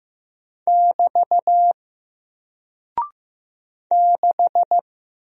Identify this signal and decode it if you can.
Morse code, heard as a keyed tone, about 15 wpm, 700 Hz. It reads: <BT> 6
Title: Single Character (Letter:Number:Punctuation:Prosign) Alt 15wpm